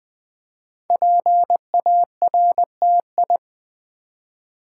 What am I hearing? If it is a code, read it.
Morse code, heard as a keyed tone, about 20 wpm, 700 Hz. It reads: PARTI
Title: Build A Word - 5 Letters At A Time - 12-Letter Words - 20wpm